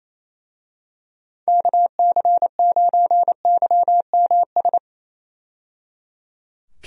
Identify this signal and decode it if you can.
Morse code, heard as a keyed tone, about 27 wpm, 700 Hz. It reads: KC9YMS
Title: USA Call Signs - 6 Characters - 28wpm